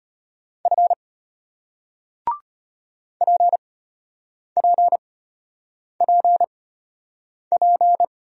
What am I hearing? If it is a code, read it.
Morse code, heard as a keyed tone, about 34 wpm, spaced as for 8 wpm, 700 Hz. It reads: F PPPP